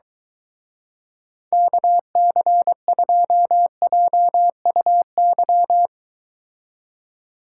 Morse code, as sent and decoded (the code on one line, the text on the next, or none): -.- -.-. ..--- .--- ..- -.--
KC2JUY